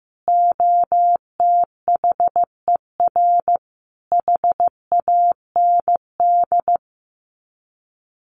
Morse code, heard as a keyed tone, about 15 wpm, 700 Hz. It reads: OTHER HAND
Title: Sets of 4 Words 15wpm